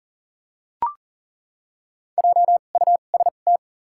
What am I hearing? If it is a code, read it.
Morse code, heard as a keyed tone, about 40 wpm, spaced as for 29 wpm, 700 Hz. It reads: JUST
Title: 2x Character Spacing - Top 200 Words - 40wpm